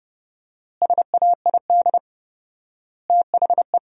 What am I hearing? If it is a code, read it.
Morse code, heard as a keyed tone, about 32 wpm, 700 Hz. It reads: SAID THE
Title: 4x Word-Spacing - Sentences from Top 500 Words - 30wpm